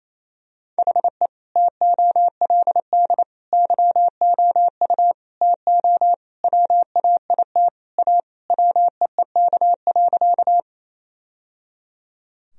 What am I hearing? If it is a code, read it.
Morse code, heard as a keyed tone, about 28 wpm, 700 Hz. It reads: HE TOLD YOU TO WAIT A WEEK.
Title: Sentences from Top 1000 Words - Review Part 2 - 28wpm